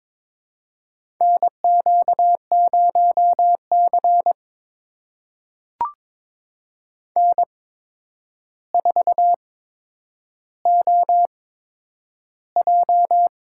Morse code, heard as a keyed tone, about 21 wpm, spaced as for 9 wpm, 700 Hz. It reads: NQ0C N4OJ
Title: USA Call Signs - 4 Characters - 8x Character Spacing - 22wpm